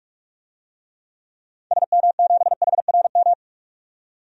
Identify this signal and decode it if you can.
Morse code, heard as a keyed tone, about 46 wpm, 700 Hz. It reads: SM7HRM
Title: DXCC-100 Call Signs - 6 Characters - 45wpm